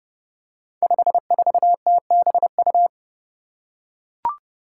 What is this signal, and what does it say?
Morse code, heard as a keyed tone, about 28 wpm, 700 Hz. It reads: 54TBU
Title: Five Challenging Letter-Number Combinations 30wpm